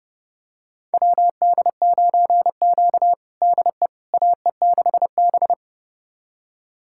Morse code, heard as a keyed tone, about 30 wpm, 700 Hz. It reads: WD9Q DE AE6B
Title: USA Callsign DE Callsign - 4 Characters - 30wpm